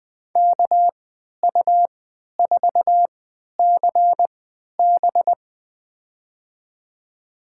Morse code, heard as a keyed tone, about 20 wpm, spaced as for 11 wpm, 700 Hz. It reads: KU4CB
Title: USA Call Signs - 5 Characters - 3x Character Spacing - 20wpm